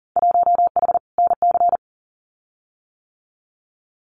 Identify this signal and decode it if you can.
Morse code, heard as a keyed tone, about 40 wpm, 700 Hz. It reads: 1H NC
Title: Winter Field Day Exchanges 40wpm